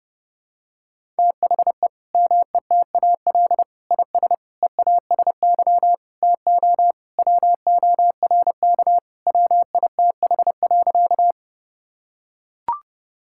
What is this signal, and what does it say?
Morse code, heard as a keyed tone, about 30 wpm, 700 Hz. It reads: THE METAL IS EASY TO WORK WITH.
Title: Sentences from Top 1000 Words - Review Part 6 - 30wpm